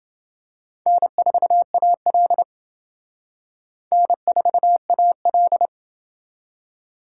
Morse code, heard as a keyed tone, about 28 wpm, 700 Hz. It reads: N4AL N4AL